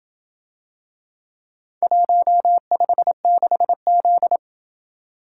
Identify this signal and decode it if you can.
Morse code, heard as a keyed tone, about 27 wpm, 700 Hz. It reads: 156Z